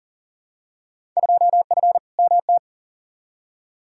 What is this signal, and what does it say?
Morse code, heard as a keyed tone, about 40 wpm, 700 Hz. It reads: 2F MT
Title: Field Day Exchanges Encore - 40wpm